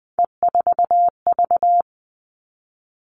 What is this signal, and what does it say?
Morse code, heard as a keyed tone, about 20 wpm, 700 Hz. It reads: E4V
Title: All Cut Numbers - 3-Digit 20wpm